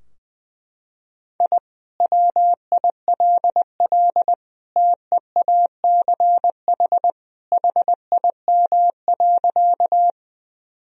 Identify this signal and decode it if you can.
Morse code, heard as a keyed tone, about 21 wpm, 700 Hz. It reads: I WILL TEACH HIM.
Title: Sentences from Top 1000 Words - Review Part 2 - 20wpm